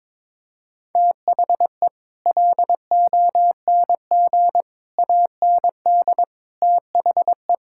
Morse code, heard as a keyed tone, about 22 wpm, 700 Hz. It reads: THE LONG AND THE
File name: Sets of 9 Words - 22wpm